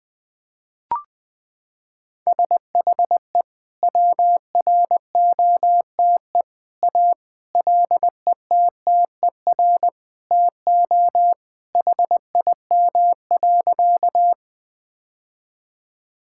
Morse code, heard as a keyed tone, about 20 wpm, 700 Hz. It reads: SHE WROTE A LETTER TO HIM.
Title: Sentences with 2-Word Phrases - 20wpm